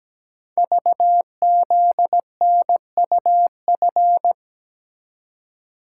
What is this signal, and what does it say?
Morse code, heard as a keyed tone, about 17 wpm, 700 Hz. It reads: VZNUF
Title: Code Groups - 5 Letter:Numbers - 17wpm